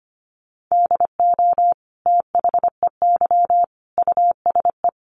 Morse code, heard as a keyed tone, about 25 wpm, 700 Hz. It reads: DO THEY USE
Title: Sentences from Top 1300 Words 25wpm